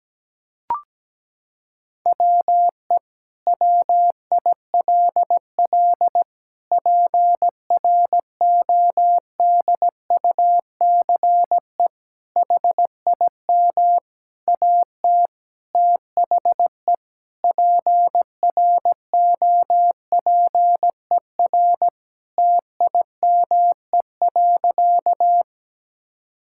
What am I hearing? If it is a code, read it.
Morse code, heard as a keyed tone, about 17 wpm, 700 Hz. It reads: WE WILL PRODUCE HIM AT THE PROPER TIME.
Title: Sentences from Top 1000 Words - Review Part 5 - 17wpm